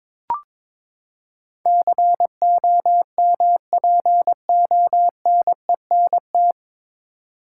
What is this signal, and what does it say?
Morse code, heard as a keyed tone, about 22 wpm, 700 Hz. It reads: COMPONENT
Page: Morse Code Ninja practice 22wpm